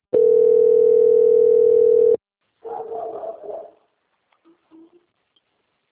Ringback tone (North America), 440 + 480 Hz.
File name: out- - - - - .15995